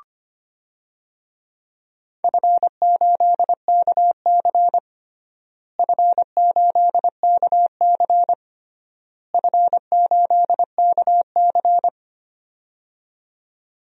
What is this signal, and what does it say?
Morse code, heard as a keyed tone, about 25 wpm, 700 Hz. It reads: F8KC F8KC F8KC
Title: DXCC-100 Call Signs - 4 Characters - Repeated 3x - 25wpm